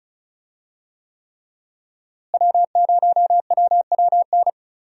Morse code, heard as a keyed tone, about 36 wpm, 700 Hz. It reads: W0WWN
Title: USA Call Signs - 5 Characters - Repeated 3x - 35wpm